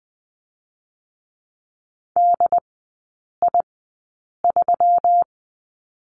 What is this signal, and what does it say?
Morse code, heard as a keyed tone, about 20 wpm, spaced as for 8 wpm, 700 Hz. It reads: DI3
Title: 152 Lesson 25e - Letter p - Three Characters 20wpm